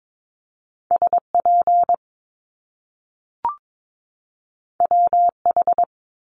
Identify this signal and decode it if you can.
Morse code, heard as a keyed tone, about 22 wpm, 700 Hz. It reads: SP WH
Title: Top Two Letter Combinations in Words 22wpm